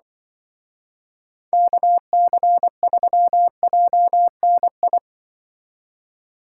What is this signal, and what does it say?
Morse code, heard as a keyed tone, about 25 wpm, 700 Hz. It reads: KC3JNI
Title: Speed Racing - USA 2x3 Call Signs 20wpm